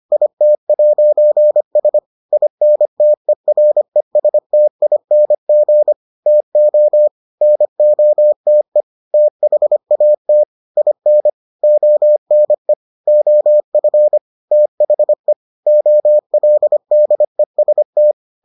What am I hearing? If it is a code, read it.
Morse code, heard as a keyed tone, about 25 wpm, 600 Hz. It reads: IT'S INTERESTING TO NOTE THAT IN ONE OF THE OLDEST